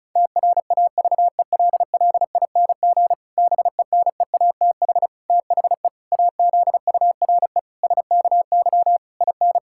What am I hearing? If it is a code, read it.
Morse code, heard as a keyed tone, about 34 wpm, 700 Hz. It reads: TRAVELLING BENEATH THE AZURE SKY IN
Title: Pangram Challenge v2 - 35wpm